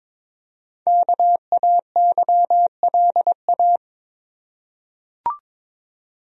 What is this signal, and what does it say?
Morse code, heard as a keyed tone, about 22 wpm, 700 Hz. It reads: KAYLA
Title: USA Common Names - Max 5 Letters - 22wpm